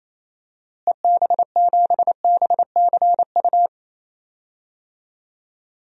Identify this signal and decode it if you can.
Morse code, heard as a keyed tone, about 28 wpm, 700 Hz. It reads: EB7BCU
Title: DXCC-100 Call Signs - 6 Characters - Repeated 3x - 28wpm